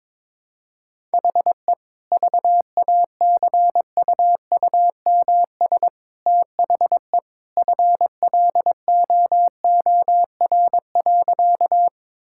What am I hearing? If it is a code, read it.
Morse code, heard as a keyed tone, about 21 wpm, 700 Hz. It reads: HE VACUUMS THE FLOOR.
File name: practice / 22wpm / Easy Sentences with Hints - 22wpm